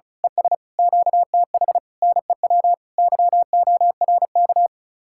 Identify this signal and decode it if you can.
Morse code, heard as a keyed tone, about 33 wpm, 700 Hz. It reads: ES QTH NEW YORK
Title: Speed-Racing Common QSO Phrases - 35wpm